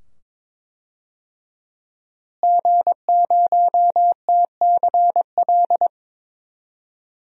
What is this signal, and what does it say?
Morse code, heard as a keyed tone, about 23 wpm, 700 Hz. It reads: G0TCL
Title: DXCC-100 Call Signs - 5 Characters - Repeated 2x - 22wpm